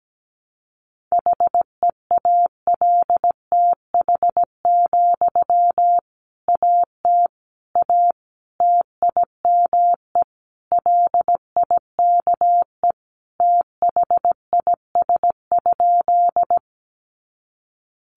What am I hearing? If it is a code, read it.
Morse code, heard as a keyed tone, about 17 wpm, 700 Hz. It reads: HEALTH, AT A TIME LIKE THIS?
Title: Sentences from Top 600 Words 17wpm